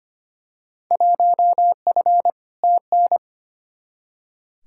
Morse code, heard as a keyed tone, about 25 wpm, 700 Hz. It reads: 1F TN
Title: Field Day Exchanges - 25wpm